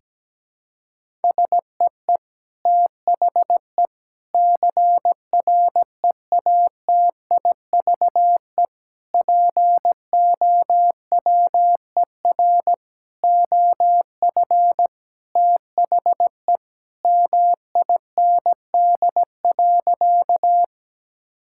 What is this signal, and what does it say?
Morse code, heard as a keyed tone, about 17 wpm, 700 Hz. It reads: SEE THE CREATIVE POWER OF THE MIND.
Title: Sentences from Top 1000 Words - Review Part 5 - 17wpm